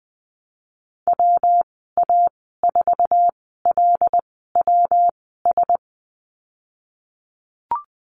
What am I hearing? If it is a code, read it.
Morse code, heard as a keyed tone, about 20 wpm, spaced as for 14 wpm, 700 Hz. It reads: WA4LWS
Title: 128 Lesson 21g - Number 6 - Call Signs 20wpm